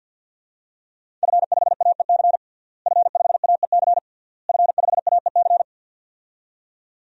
Morse code, heard as a keyed tone, about 45 wpm, 700 Hz. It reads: F5AEC F5AEC F5AEC
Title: DXCC-100 Call Signs - 5 Characters - Repeated 3x - 50wpm